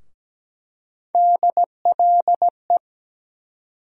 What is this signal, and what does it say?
Morse code, heard as a keyed tone, about 17 wpm, 700 Hz. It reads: DLE
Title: USA Call Sign Postfixes - 3 Letters 17wpm